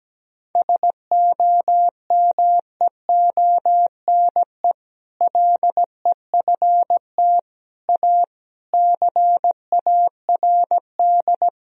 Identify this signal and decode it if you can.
Morse code, heard as a keyed tone, about 17 wpm, 700 Hz. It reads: SOMEONE LEFT A CARD
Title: Sentences from Top 1000 Words - Review Part 4 - 17wpm